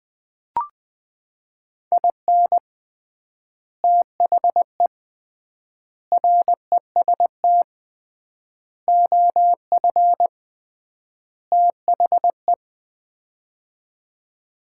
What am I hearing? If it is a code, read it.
Morse code, heard as a keyed tone, about 20 wpm, 700 Hz. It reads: IN THE REST OF THE
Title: 3x Word-Spacing - Sets of 5 Words - 20wpm